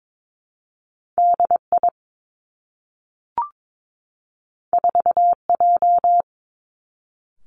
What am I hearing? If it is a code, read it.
Morse code, heard as a keyed tone, about 22 wpm, 700 Hz. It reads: D I 4 J
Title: Two Challenging Letter-Number Combinations 22wpm